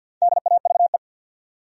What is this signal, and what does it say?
Morse code, heard as a keyed tone, about 51 wpm, 700 Hz. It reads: DAVE
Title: USA Common Names - 50wpm